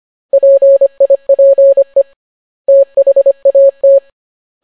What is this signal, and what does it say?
Morse code, heard as a keyed tone, about 25 wpm, 550 Hz. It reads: PIPE THAT